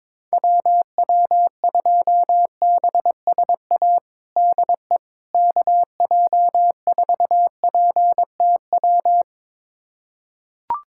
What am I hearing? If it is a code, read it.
Morse code, heard as a keyed tone, about 22 wpm, 700 Hz. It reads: WW2BSA DE KJ4PTW
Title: USA Callsign DE Callsign - 6 Characters - 22wpm